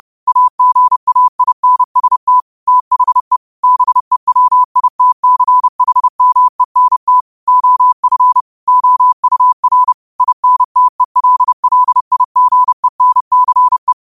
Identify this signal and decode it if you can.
Morse code, heard as a keyed tone, about 30 wpm, 1000 Hz. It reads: AGAINST THE BEWITCHMENT OF OUR INTELLIGENCE
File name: qotdfiles / 30wpm